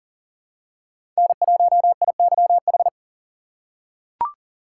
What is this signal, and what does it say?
Morse code, heard as a keyed tone, about 38 wpm, 700 Hz. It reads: N1IYH
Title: USA Call Signs - 5 Characters - 3x Character Spacing - 40wpm